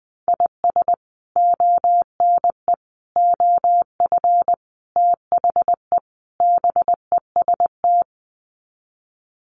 Morse code, heard as a keyed tone, about 20 wpm, 700 Hz. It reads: IS ONE OF THE BEST
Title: Sets of 5 Words 20wpm